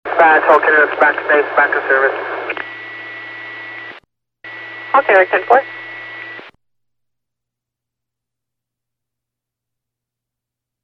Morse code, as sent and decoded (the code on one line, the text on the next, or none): none